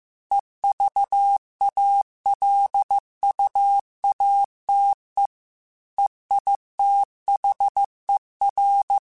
Morse code, heard as a keyed tone, about 15 wpm, 800 Hz. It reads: EVALUATE EITHER
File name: Technical Subjects / Technical Subjects-mp3s / AND Gate @13